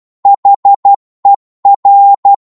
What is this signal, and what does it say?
Morse code, heard as a keyed tone, about 12 wpm, 800 Hz. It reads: HER
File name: 12wpm